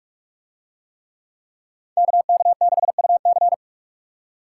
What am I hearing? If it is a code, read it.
Morse code, heard as a keyed tone, about 49 wpm, 700 Hz. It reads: KK6UC